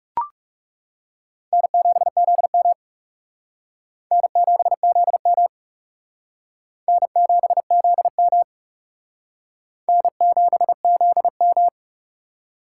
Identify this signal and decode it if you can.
Morse code, heard as a keyed tone, about 36 wpm, 700 Hz. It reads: N7ZM N7ZM N7ZM N7ZM
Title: Speed Racing - USA 1x2 Call Signs 30wpm